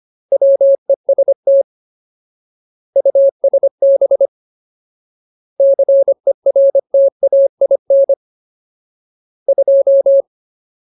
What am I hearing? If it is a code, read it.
Morse code, heard as a keyed tone, about 25 wpm, 550 Hz. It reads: WEST USB CERTAIN 2
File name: Ultimate Word List / 25WPM Ultimate